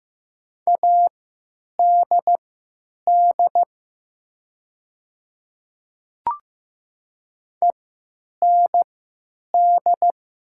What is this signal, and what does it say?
Morse code, heard as a keyed tone, about 15 wpm, spaced as for 9 wpm, 700 Hz. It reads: ADD END
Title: 3x Character Spacing - Top 200 Words - 15wpm